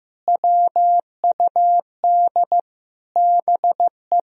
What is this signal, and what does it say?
Morse code, heard as a keyed tone, about 15 wpm, 700 Hz. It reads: WUD BE
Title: Sentences from Top 300 Words and Abbreviations 15wpm